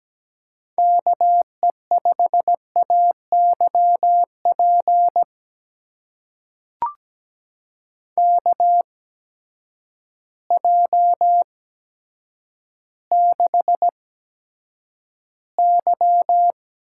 Morse code, heard as a keyed tone, about 17 wpm, spaced as for 9 wpm, 700 Hz. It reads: KE5AYP KJ6Y
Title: USA Call Signs - 6 Characters - 8x Character Spacing - 17wpm